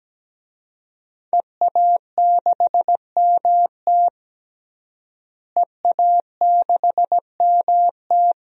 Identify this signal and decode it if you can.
Morse code, heard as a keyed tone, about 17 wpm, 700 Hz. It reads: EA6MT EA6MT
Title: DXCC-100 Call Signs - 5 Characters - Repeated 2x - 17wpm